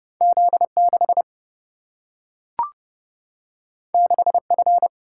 Morse code, heard as a keyed tone, about 30 wpm, 700 Hz. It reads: Z 6 6 F
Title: Single-Letter Single-Number 30wpm